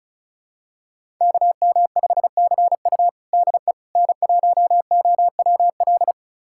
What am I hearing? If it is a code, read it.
Morse code, heard as a keyed tone, about 36 wpm, 700 Hz. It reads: KM5CU DE N1OWL
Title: USA Callsign DE Callsign - 5 Characters - 35wpm